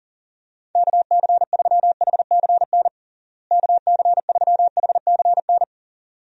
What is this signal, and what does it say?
Morse code, heard as a keyed tone, about 40 wpm, 700 Hz. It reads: KC3HCN KC3HCN
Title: USA Call Signs - 6 Characters - Repeated 2x - 40wpm